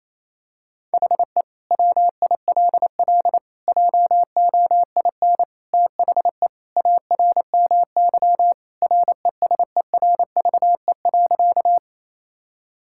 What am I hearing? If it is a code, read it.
Morse code, heard as a keyed tone, about 28 wpm, 700 Hz. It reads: HE WILL JOIN THE ARMY RESERVE.
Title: Sentences from Top 1000 Words - Review Part 6 - 28wpm